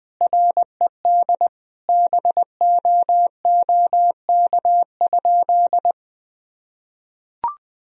Morse code, heard as a keyed tone, about 20 wpm, 700 Hz. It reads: RED BOOK?
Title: Sentences from Top 1000 Words - Review Part 2 - 20wpm